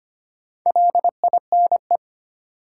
Morse code, heard as a keyed tone, about 25 wpm, 700 Hz. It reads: LINE
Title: Sentences from Top 1000 Words - Review Part 1 - New Word Drill - 25wpm